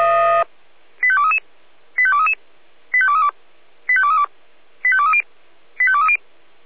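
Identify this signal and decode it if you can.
Five-tone selective call (ZVEI), heard as quick runs of five tones, tones of about 70 ms. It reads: 85233 85233 85232 85232 85230 85230
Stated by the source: ZVEI five-tone call received by SDR.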